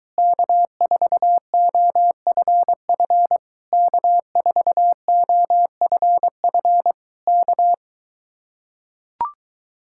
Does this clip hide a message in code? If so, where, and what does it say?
Yes: Morse code from 0.2 to 7.7 s, heard as a keyed tone, about 23 wpm, 700 Hz. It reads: K4OFF K4OFF K